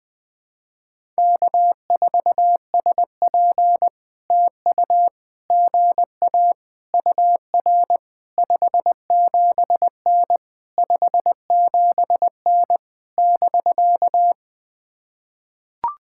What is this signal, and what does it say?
Morse code, heard as a keyed tone, about 21 wpm, 700 Hz. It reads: K4SP TU GA UR 57N 57N <BK>
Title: POTA QSO Practice - 20wpm